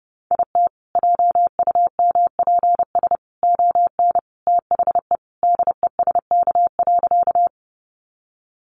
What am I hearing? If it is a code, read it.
Morse code, heard as a keyed tone, about 28 wpm, 700 Hz. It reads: IT JUMPS ON THE DESK.
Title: Additional Easy Phrases 30wpm